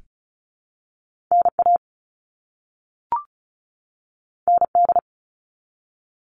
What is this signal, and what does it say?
Morse code, heard as a keyed tone, about 36 wpm, 700 Hz. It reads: NA ND
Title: All Cut Numbers - 2-Digit 35wpm